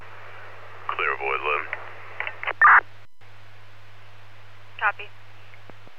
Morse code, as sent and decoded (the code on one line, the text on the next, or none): none